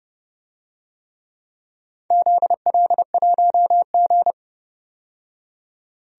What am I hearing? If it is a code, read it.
Morse code, heard as a keyed tone, about 31 wpm, 700 Hz. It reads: ZL1G
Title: DXCC-100 Call Signs - 4 Characters - 30wpm